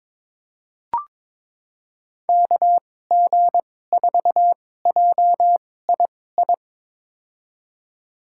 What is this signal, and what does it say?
Morse code, heard as a keyed tone, about 22 wpm, spaced as for 16 wpm, 700 Hz. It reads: KG4JII
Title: USA Call Signs - 6 Characters - 2x Character Spacing - 22wpm